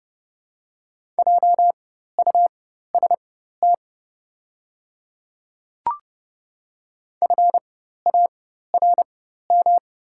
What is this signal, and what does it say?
Morse code, heard as a keyed tone, about 30 wpm, spaced as for 14 wpm, 700 Hz. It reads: JUST FARM
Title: 4x Character Spacing - Top 500 Words - 30wpm